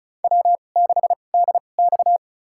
Morse code, heard as a keyed tone, about 34 wpm, spaced as for 25 wpm, 700 Hz. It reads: W6DX
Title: USA Call Signs - 4 Characters - 2x Character Spacing - 35wpm